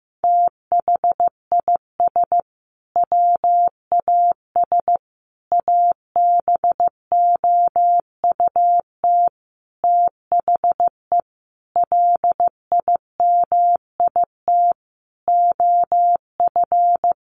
Morse code, heard as a keyed tone, about 15 wpm, 700 Hz. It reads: THIS WAS ABOUT THE LIMIT OF
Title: Sentences from Top 1500 Words 15wpm